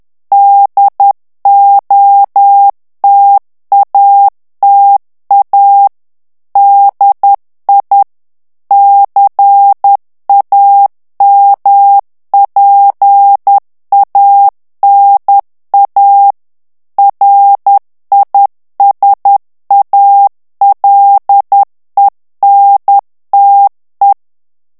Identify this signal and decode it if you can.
Morse code, heard as a keyed tone, about 10 wpm, 800 Hz. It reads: DOTATA DI CAMPANA RISALENTE
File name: ITALIEN texte clair 1 vitesse 10 mots minute